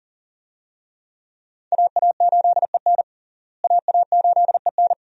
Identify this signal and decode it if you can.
Morse code, heard as a keyed tone, about 40 wpm, 700 Hz. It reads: AA8EN AA8EN
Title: USA Call Signs - 5 Characters - Repeated 2x - 40wpm